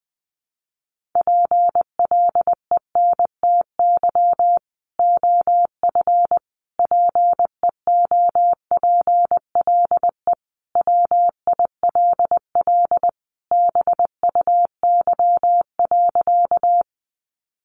Morse code, heard as a keyed tone, about 20 wpm, 700 Hz. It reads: PLENTY OF PEOPLE WILL BUY.
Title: Sentences from Top 1300 Words 20wpm